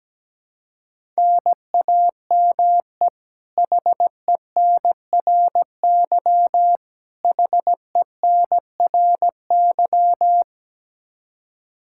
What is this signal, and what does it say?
Morse code, heard as a keyed tone, about 17 wpm, 700 Hz. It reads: NAME HENRY HENRY
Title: Practice QSO - Exchange 1 - Beginning - 17wpm